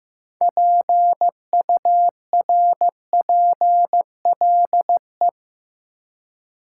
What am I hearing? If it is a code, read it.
Morse code, heard as a keyed tone, about 15 wpm, 700 Hz. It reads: PURPLE
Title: Six Letter Words - 15wpm